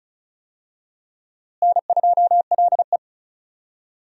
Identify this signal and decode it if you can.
Morse code, heard as a keyed tone, about 34 wpm, 700 Hz. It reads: N2LE
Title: USA Call Signs - 4 Characters - Repeated 3x - 35wpm